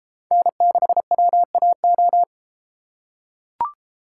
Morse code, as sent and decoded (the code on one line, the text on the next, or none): -. -.... .-- .- ---
N6WAO